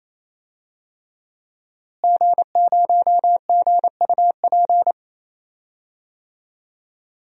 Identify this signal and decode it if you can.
Morse code, heard as a keyed tone, about 27 wpm, 700 Hz. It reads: G0GUP
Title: DXCC-100 Call Signs - 5 Characters - Repeated 3x - 28wpm